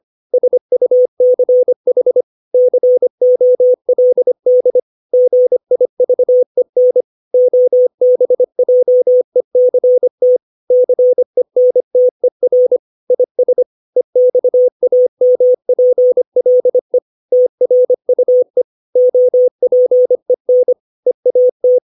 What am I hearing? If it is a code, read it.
Morse code, heard as a keyed tone, about 25 wpm, 500 Hz. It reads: SUCH COLD GIVEN OBJECT CENTER IS EXAMPLE TRUE OPEN EAT
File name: 632 words / 632 Words 25WPM